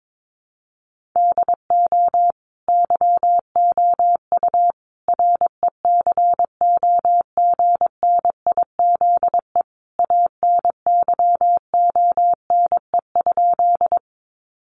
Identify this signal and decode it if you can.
Morse code, heard as a keyed tone, about 22 wpm, 700 Hz. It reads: DO YOU RECOGNIZE ANYONE?
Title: Sentences from Top 1400 Words 22wpm